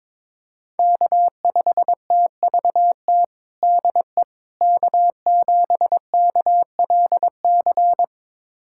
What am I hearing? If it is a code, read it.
Morse code, heard as a keyed tone, about 22 wpm, 700 Hz. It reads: K5TVT DE K7KLC
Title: USA Callsign DE Callsign - 5 Characters - 22wpm